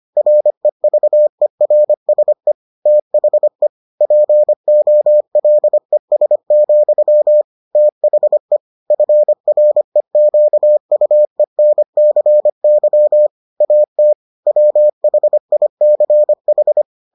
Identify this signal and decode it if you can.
Morse code, heard as a keyed tone, about 25 wpm, 600 Hz. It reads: REVERSE THE POLES, THE FREQUENCY AT WHICH